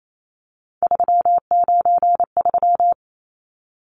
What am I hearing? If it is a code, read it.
Morse code, heard as a keyed tone, about 28 wpm, 700 Hz. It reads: 393
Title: Numbers - 3-Digit 28wpm